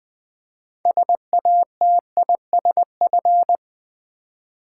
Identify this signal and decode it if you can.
Morse code, heard as a keyed tone, about 20 wpm, 700 Hz. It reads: SATISF
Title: Build A Word - 3 Letters At A Time - 9-Letter Words - 20wpm